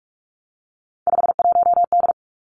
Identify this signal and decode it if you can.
Morse code, heard as a keyed tone, about 46 wpm, 700 Hz. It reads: 51D
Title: Three Challenging Letter-Number Combinations 45wpm